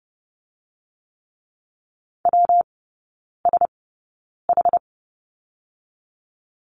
Morse code, heard as a keyed tone, about 30 wpm, spaced as for 9 wpm, 700 Hz. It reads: WSH
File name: Lesson 19e - Letter w - Three Characters 30wpm